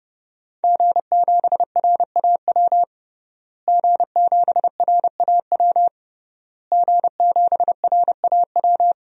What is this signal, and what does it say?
Morse code, heard as a keyed tone, about 32 wpm, 700 Hz. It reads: G7RAW G7RAW G7RAW
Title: DXCC-100 Call Signs - 5 Characters - Repeated 3x - 30wpm